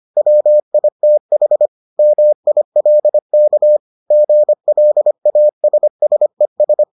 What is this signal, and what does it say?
Morse code, heard as a keyed tone, about 25 wpm, 600 Hz. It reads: WITH MILK GLASSES